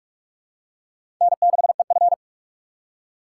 Morse code, heard as a keyed tone, about 46 wpm, 700 Hz. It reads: N6EF
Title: USA Call Signs - 3x Character Spacing - 45wpm